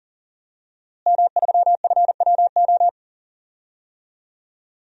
Morse code, heard as a keyed tone, about 40 wpm, 700 Hz. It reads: M3FWO
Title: DXCC-100 Call Signs - 5 Characters - Repeated 3x - 40wpm